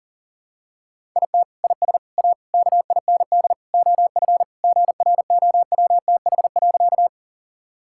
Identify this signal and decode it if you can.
Morse code, heard as a keyed tone, about 40 wpm, 700 Hz. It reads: IT IS A KIND OF GROWTH.
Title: Sentences from Top 1000 Words - Review Part 5 - 40wpm